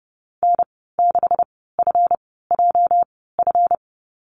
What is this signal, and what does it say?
Morse code, heard as a keyed tone, about 30 wpm, spaced as for 17 wpm, 700 Hz. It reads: N6FJF
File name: Lesson 37g - Letter x - Call Signs 30wpm